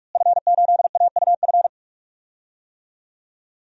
Morse code, heard as a keyed tone, about 46 wpm, 700 Hz. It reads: F8AUF
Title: DXCC-100 Call Signs - 5 Characters - Repeated 2x - 45wpm